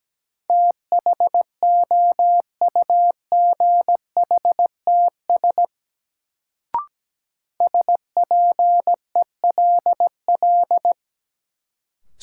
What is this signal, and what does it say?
Morse code, heard as a keyed tone, about 17 wpm, 700 Hz. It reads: THOUGHTS SPELL
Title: Sentences from Top 1000 Words - Review Part 1 - New Word Drill - 17wpm